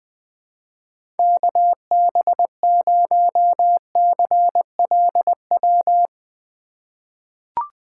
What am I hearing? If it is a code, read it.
Morse code, heard as a keyed tone, about 20 wpm, 700 Hz. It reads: KB0CLW
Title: USA Call Signs - 3x Character Spacing - 20wpm